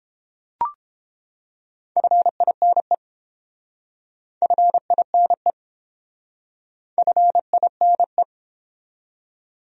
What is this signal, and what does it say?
Morse code, heard as a keyed tone, about 30 wpm, 700 Hz. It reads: FINE FINE FINE